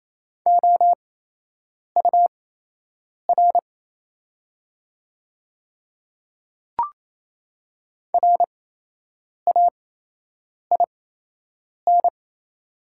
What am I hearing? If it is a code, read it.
Morse code, heard as a keyed tone, about 28 wpm, spaced as for 8 wpm, 700 Hz. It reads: OUR RAIN
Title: 8x Character Spacing - Top 300 Words - 28wpm